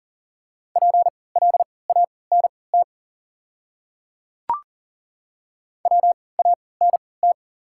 Morse code, heard as a keyed tone, about 40 wpm, spaced as for 23 wpm, 690 Hz. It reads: PLANT WANT
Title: 3x Character Spacing - Top 500 Words - 40wpm